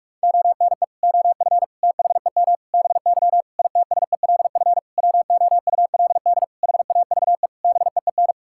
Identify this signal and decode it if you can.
Morse code, heard as a keyed tone, about 46 wpm, 700 Hz. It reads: ONE OF THEM BY ITSELF WOULD HAVE BEEN
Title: Sentences from Top 1000 Words - Review Part 5 - 45wpm